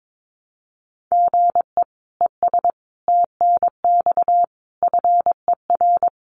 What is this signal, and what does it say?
Morse code, heard as a keyed tone, about 22 wpm, 700 Hz. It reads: GE ES TNX FER
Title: Common QSO Phrases Encore 22wpm